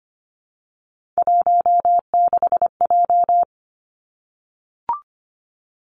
Morse code, heard as a keyed tone, about 25 wpm, 700 Hz. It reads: 16J
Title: Three Challenging Letter-Number Combinations 25wpm